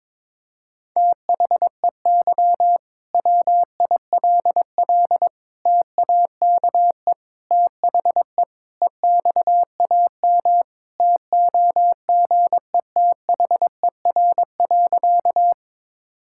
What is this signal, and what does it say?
Morse code, heard as a keyed tone, about 22 wpm, 700 Hz. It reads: THEY WILL TAKE THE EXAM TOGETHER.
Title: Sentences from Top 1000 Words - Review Part 5 - 22wpm